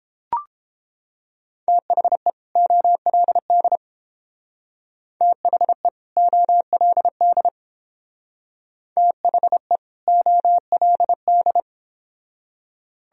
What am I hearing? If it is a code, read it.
Morse code, heard as a keyed tone, about 31 wpm, 700 Hz. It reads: THE OLD THE OLD THE OLD